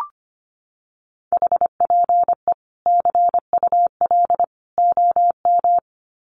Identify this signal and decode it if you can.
Morse code, heard as a keyed tone, about 25 wpm, 700 Hz. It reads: HPE CUL OM
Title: Common QSO Phrases 25wpm